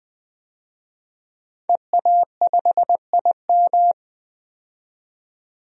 Morse code, heard as a keyed tone, about 20 wpm, 700 Hz. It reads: EA5IM
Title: DXCC-100 Call Signs - 5 Characters - 20wpm